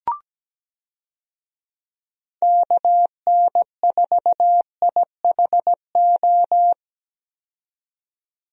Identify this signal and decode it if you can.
Morse code, heard as a keyed tone, about 17 wpm, 700 Hz. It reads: KN4IHO
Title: USA Call Signs - 6 Characters - 17wpm